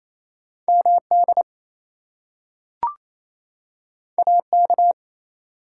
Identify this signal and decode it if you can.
Morse code, heard as a keyed tone, about 28 wpm, 700 Hz. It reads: MD AK
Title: US States and Canadian Provinces 28wpm